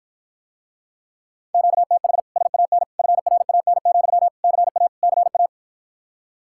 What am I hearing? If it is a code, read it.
Morse code, heard as a keyed tone, about 52 wpm, 700 Hz. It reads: QTH SAN FRAN, CA CA